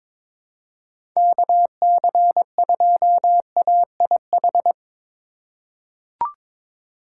Morse code, heard as a keyed tone, about 22 wpm, 700 Hz. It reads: KC2AIH
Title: USA Call Signs - 6 Characters - 4x Character Spacing - 22wpm